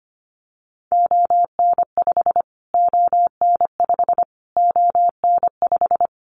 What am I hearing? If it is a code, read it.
Morse code, heard as a keyed tone, about 25 wpm, 700 Hz. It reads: ON5 ON5 ON5
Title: Callsign Prefixes for Top 25 DXCC - Repeated 3x - 25wpm